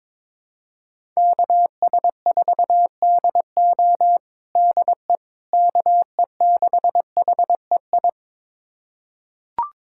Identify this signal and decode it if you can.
Morse code, heard as a keyed tone, about 22 wpm, 700 Hz. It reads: KS4DO DE KE6HEI
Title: USA Callsign DE Callsign 22wpm